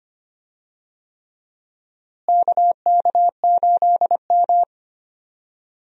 Morse code, heard as a keyed tone, about 25 wpm, 700 Hz. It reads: KK8M
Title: USA Call Signs - 4 Characters - Repeated 3x - 25wpm